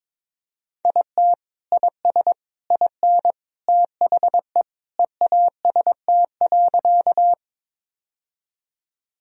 Morse code, heard as a keyed tone, about 22 wpm, 700 Hz. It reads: IT IS IN THE EAST.
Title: Sentences from Top 1000 Words - Review Part 3 - 22wpm